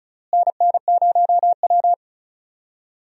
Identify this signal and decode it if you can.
Morse code, heard as a keyed tone, about 36 wpm, 700 Hz. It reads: NN0W
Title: USA Call Signs - 4 Characters - Repeated 3x - 35wpm